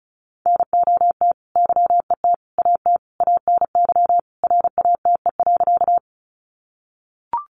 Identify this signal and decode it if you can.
Morse code, heard as a keyed tone, about 34 wpm, 700 Hz. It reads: NOT YET AT ANY RATE.
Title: Sentences from Top 700 Words 35wpm